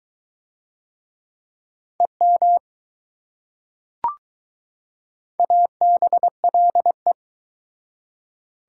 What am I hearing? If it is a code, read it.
Morse code, heard as a keyed tone, about 20 wpm, 700 Hz. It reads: EM ABLE